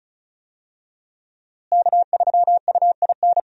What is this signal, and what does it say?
Morse code, heard as a keyed tone, about 36 wpm, 700 Hz. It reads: K3UIN
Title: USA Call Signs - 5 Characters - Repeated 3x - 35wpm